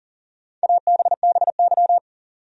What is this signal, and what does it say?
Morse code, heard as a keyed tone, about 40 wpm, 700 Hz. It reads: ABBY
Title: USA Common Names - 40wpm